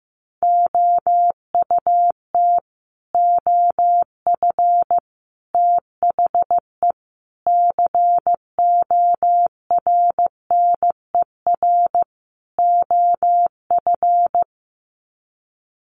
Morse code, heard as a keyed tone, about 15 wpm, 700 Hz. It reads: OUT OF THE CORNER OF
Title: Sets of 5 Words 15wpm